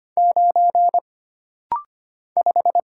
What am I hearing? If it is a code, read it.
Morse code, heard as a keyed tone, about 25 wpm, 700 Hz. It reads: 9 5
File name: Single Character (Letter:Number:Punctuation:Prosign) - Rapid-Fire - 25wpm